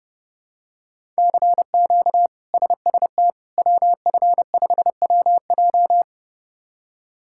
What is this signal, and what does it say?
Morse code, heard as a keyed tone, about 32 wpm, 700 Hz. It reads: CQ SST WF5WJ
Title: K1USN SST QSO Practice - 30wpm